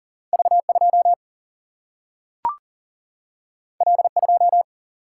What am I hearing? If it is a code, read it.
Morse code, heard as a keyed tone, about 39 wpm, 700 Hz. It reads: V 2 L 2
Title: Single-Letter Single-Number 40wpm